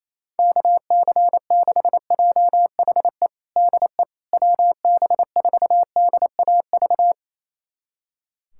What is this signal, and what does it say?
Morse code, heard as a keyed tone, about 27 wpm, 700 Hz. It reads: KC6JHE DE WB4DAV
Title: USA Callsign DE Callsign 28wpm